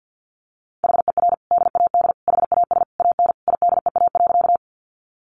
Morse code, heard as a keyed tone, about 50 wpm, 700 Hz. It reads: HER DAD HAS AN IDEA.
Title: Easy Phrases 50wpm